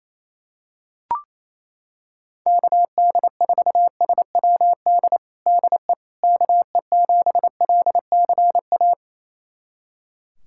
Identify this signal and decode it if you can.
Morse code, heard as a keyed tone, about 28 wpm, 700 Hz. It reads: KD4SWD DE KE7LCA
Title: USA Callsign DE Callsign - 6 Characters - 28wpm